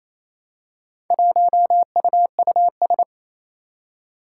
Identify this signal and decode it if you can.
Morse code, heard as a keyed tone, about 27 wpm, 700 Hz. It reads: 1UUS
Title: Four Challenging Letter-Number Combinations 28wpm